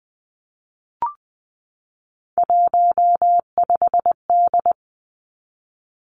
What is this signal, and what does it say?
Morse code, heard as a keyed tone, about 20 wpm, 700 Hz. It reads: 15D
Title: Three Challenging Letter-Number Combinations 20wpm